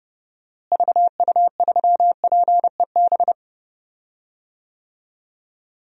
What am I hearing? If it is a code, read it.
Morse code, heard as a keyed tone, about 32 wpm, 700 Hz. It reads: VU3PEB
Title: DXCC-100 Call Signs - 6 Characters - Repeated 3x - 30wpm